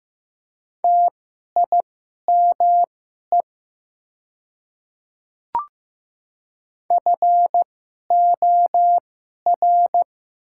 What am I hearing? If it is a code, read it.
Morse code, heard as a keyed tone, about 15 wpm, spaced as for 11 wpm, 700 Hz. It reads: TIME FOR
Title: 2x Character Spacing - Top 100 Words - 15wpm